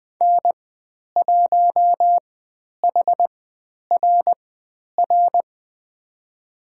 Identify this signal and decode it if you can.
Morse code, heard as a keyed tone, about 20 wpm, spaced as for 10 wpm, 700 Hz. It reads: N1HRR
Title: 109 Lesson 18g - Letter m - Call Signs 20wpm